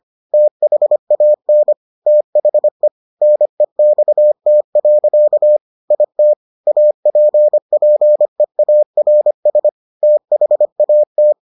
Morse code, heard as a keyed tone, about 25 wpm, 600 Hz. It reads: THAN THE NEXT. IT APPEARS THAT